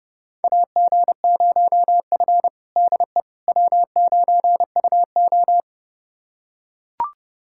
Morse code, heard as a keyed tone, about 30 wpm, 700 Hz. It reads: AG0F DE W9UO
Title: USA Callsign DE Callsign - 4 Characters - 30wpm